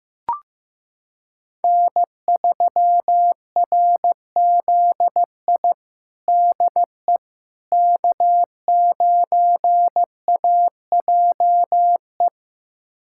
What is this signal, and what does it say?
Morse code, heard as a keyed tone, about 15 wpm, 700 Hz. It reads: N3RZI DE K9AJE
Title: USA Callsign DE Callsign - 5 Characters - 15wpm